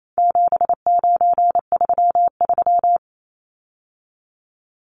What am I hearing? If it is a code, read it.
Morse code, heard as a keyed tone, about 28 wpm, 700 Hz. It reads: 7933
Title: Numbers - 4-Digit 28wpm